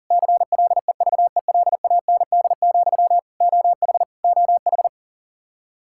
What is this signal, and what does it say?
Morse code, heard as a keyed tone, about 40 wpm, 700 Hz. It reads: CLEVELAND, OH OH
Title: Practice QSO - Exchange 1 - Beginning - 40wpm